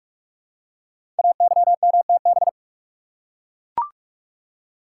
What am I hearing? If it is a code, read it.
Morse code, heard as a keyed tone, about 44 wpm, 700 Hz. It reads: AYMTB
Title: Code Groups - 5 Letters - 45wpm